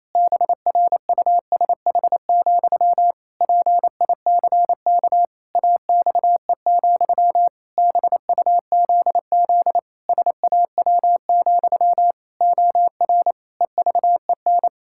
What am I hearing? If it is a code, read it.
Morse code, heard as a keyed tone, about 28 wpm, 700 Hz. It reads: BRUSH, PICK AXE, BUZZ SAW, OR EVEN
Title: Pangram Challenge v2 - 28wpm